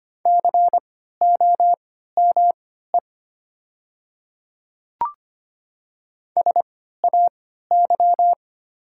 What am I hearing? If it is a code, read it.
Morse code, heard as a keyed tone, about 25 wpm, spaced as for 14 wpm, 700 Hz. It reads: COME SAY
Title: 3x Character Spacing - Top 200 Words - 25wpm